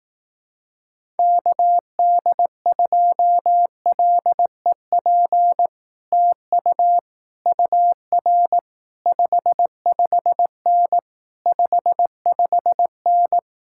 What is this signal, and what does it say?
Morse code, heard as a keyed tone, about 18 wpm, 700 Hz. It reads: KD2LEP TU UR 55N 55N
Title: Speed-Racing Common QSO Phrases Encore - 15wpm